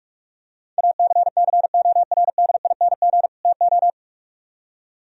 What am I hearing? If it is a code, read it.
Morse code, heard as a keyed tone, about 44 wpm, 700 Hz. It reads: ACCORDING TO